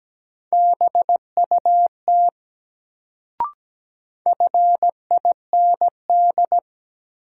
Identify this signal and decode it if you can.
Morse code, heard as a keyed tone, about 17 wpm, 700 Hz. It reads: BUT FIND
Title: Sets of 2 Words - New Word Drill - 17wpm